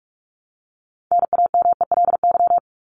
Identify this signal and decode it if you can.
Morse code, heard as a keyed tone, about 44 wpm, 700 Hz. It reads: NAMELY
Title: LY Suffix - Top 100 Words with Base Words 45wpm